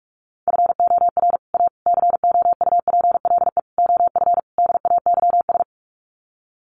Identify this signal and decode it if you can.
Morse code, heard as a keyed tone, about 44 wpm, 700 Hz. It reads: FOR A COUPLE OF DAYS
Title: Sets of 5 Words Encore2 45wpm